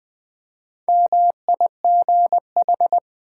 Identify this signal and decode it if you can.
Morse code, heard as a keyed tone, about 20 wpm, 700 Hz. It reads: MIGH
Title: Build A Word - 1 Letter At A Time - 5-Letter Words - 20wpm